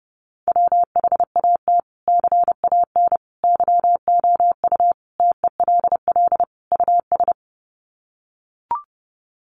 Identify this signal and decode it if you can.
Morse code, heard as a keyed tone, about 30 wpm, 700 Hz. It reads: WHAT CAN YOU TELL US
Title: Sets of 5 Words Encore 30wpm